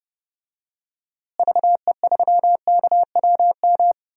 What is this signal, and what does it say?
Morse code, heard as a keyed tone, about 30 wpm, 700 Hz. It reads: VE3KWM
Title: DXCC-100 Call Signs - 6 Characters - Repeated 3x - 30wpm